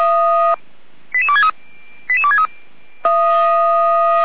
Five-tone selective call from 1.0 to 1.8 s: 88263

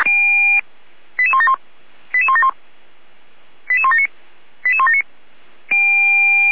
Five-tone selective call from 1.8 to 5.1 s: 88171 88179 88179

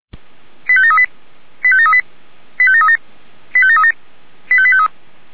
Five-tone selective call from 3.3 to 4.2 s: 85737

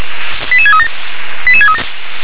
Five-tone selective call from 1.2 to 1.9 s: 88637

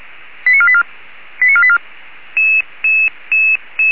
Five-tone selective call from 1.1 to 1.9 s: 89484

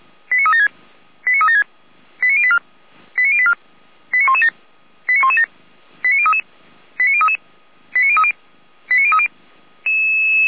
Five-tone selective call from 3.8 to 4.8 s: 89117

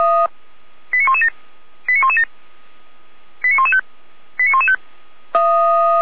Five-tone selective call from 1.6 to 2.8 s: 89117

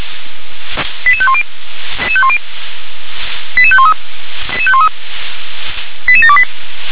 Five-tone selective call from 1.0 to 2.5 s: 88510

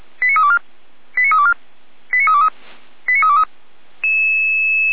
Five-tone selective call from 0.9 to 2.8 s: 89324 89323

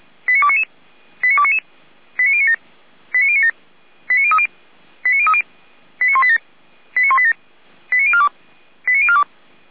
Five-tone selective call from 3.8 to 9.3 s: 89030 89030 89187 89187 89042 89042

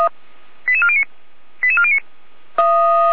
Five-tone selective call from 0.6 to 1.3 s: 88409

